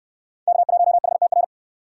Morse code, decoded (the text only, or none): K1HTR